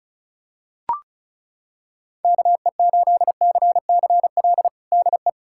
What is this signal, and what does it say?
Morse code, heard as a keyed tone, about 34 wpm, 700 Hz. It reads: KE8CCL DE
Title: USA Callsign DE Callsign - 6 Characters - 35wpm